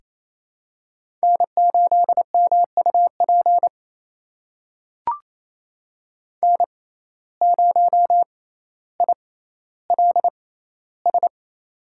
Morse code, heard as a keyed tone, about 28 wpm, spaced as for 14 wpm, 700 Hz. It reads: N8MUP N0ILS
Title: USA Call Signs - 5 Characters - 6x Character Spacing - 28wpm